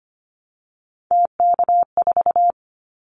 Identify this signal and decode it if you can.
Morse code, heard as a keyed tone, about 25 wpm, 700 Hz. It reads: TK4
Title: Callsign Prefixes for Top 100 DXCC - 25wpm